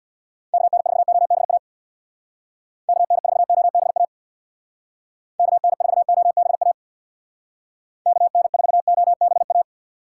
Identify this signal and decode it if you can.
Morse code, heard as a keyed tone, about 65 wpm, 700 Hz. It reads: KN4OBA KN4OBA KN4OBA KN4OBA